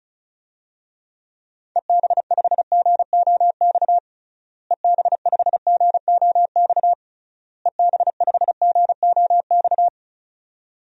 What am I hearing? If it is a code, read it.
Morse code, heard as a keyed tone, about 36 wpm, 700 Hz. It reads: EB5GOX EB5GOX EB5GOX
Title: DXCC-100 Call Signs - 6 Characters - Repeated 3x - 35wpm